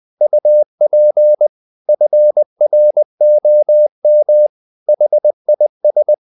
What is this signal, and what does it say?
Morse code, heard as a keyed tone, about 20 wpm, 600 Hz. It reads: UP FROM HIS
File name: the old man and the sea part 2 20wpm